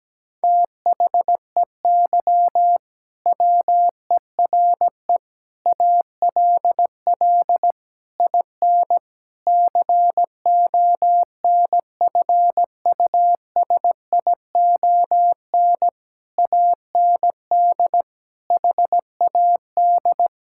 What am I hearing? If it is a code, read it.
Morse code, heard as a keyed tone, about 17 wpm, 700 Hz. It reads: THEY WERE ALL IN CONFUSION AND HAD
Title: Sentences from Top 1600 Words 17wpm